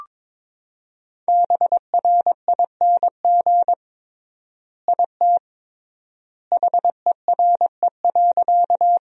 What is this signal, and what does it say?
Morse code, heard as a keyed tone, about 22 wpm, 700 Hz. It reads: BRING IT HERE.
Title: 3x Word-Spacing - Sentences from Top 500 Words - 22wpm